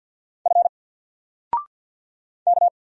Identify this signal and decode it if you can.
Morse code, heard as a keyed tone, about 50 wpm, 700 Hz. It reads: F K
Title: Single Letter-Number 50wpm